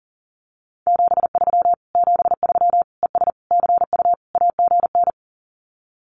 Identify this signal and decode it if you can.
Morse code, heard as a keyed tone, about 40 wpm, 700 Hz. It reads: 73 73 ES CU AGN
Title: Common QSO Phrases Encore 40wpm